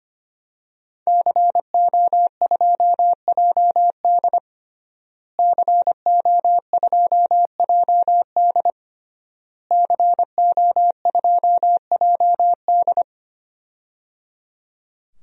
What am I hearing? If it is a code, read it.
Morse code, heard as a keyed tone, about 25 wpm, 700 Hz. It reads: CO2JD CO2JD CO2JD
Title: DXCC-100 Call Signs - 5 Characters - Repeated 3x - 25wpm